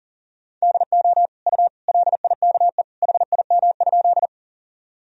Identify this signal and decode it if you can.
Morse code, heard as a keyed tone, about 40 wpm, 700 Hz. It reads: DO U LIKE HIM?
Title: Sentences from Top 100 Words and Abbreviations 40wpm